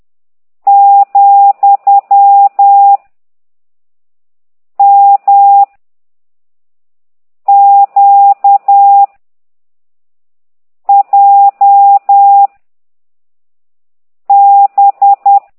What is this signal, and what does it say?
Morse code, heard as a keyed tone, about 10 wpm, 800 Hz. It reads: , M Q J B
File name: ct2idw / exerc13